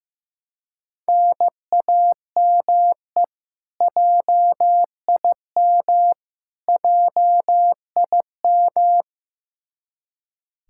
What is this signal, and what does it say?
Morse code, heard as a keyed tone, about 16 wpm, 700 Hz. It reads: NAME JIM JIM
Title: Practice QSO - Exchange 1 - Beginning - 15wpm